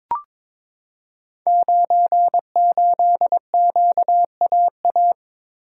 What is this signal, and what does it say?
Morse code, heard as a keyed tone, about 22 wpm, 700 Hz. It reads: 98QAA
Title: Code Groups - 5 Letter:Numbers - 22wpm